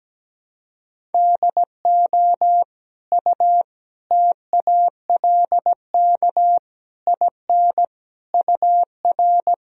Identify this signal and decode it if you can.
Morse code, heard as a keyed tone, about 17 wpm, 700 Hz. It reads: DO U TALK IN UR
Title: Sentences from Top 500 Words and Abbreviations 17wpm